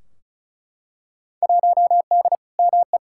Morse code, heard as a keyed tone, about 34 wpm, 700 Hz. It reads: 1D ME
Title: Field Day Exchanges Encore - 35wpm